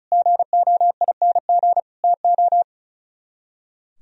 Morse code, heard as a keyed tone, about 42 wpm, 700 Hz. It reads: GOING TO
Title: Speed-Racing - Sets of 3 Words 35wpm